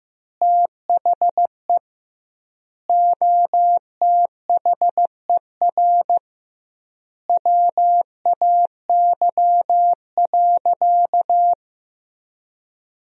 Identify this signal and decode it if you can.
Morse code, heard as a keyed tone, about 16 wpm, 700 Hz. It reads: THE OTHER WAY.
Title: 2x Word-Spacing - Sentences from Top 100 Words - 15wpm